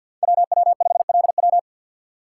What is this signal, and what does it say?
Morse code, heard as a keyed tone, about 52 wpm, 700 Hz. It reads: WW5LW
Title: USA Call Signs - 5 Characters - 8x Character Spacing - 50wpm